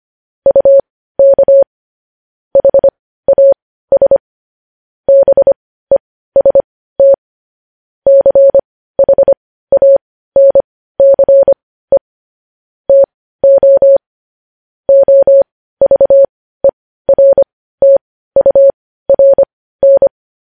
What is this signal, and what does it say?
Morse code, heard as a keyed tone, about 25 wpm, spaced as for 15 wpm, 550 Hz. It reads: UK HAS BEST CHANCE TO OVERTURN
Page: news-headlines-in-morse-code-at-15-wpm